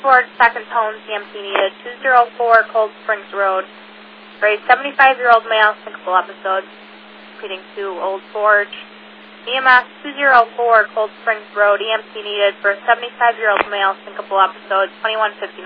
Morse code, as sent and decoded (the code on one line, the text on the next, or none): none